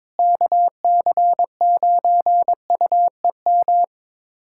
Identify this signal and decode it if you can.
Morse code, heard as a keyed tone, about 22 wpm, 700 Hz. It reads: KC9UEM
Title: USA Callsign DE Callsign - 6 Characters - 22wpm